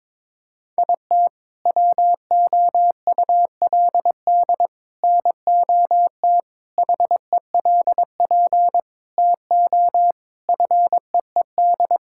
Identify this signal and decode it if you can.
Morse code, heard as a keyed tone, about 22 wpm, 700 Hz. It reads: IT WOULD NOT HELP TO FEED
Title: Sentences from Top 1600 Words 22wpm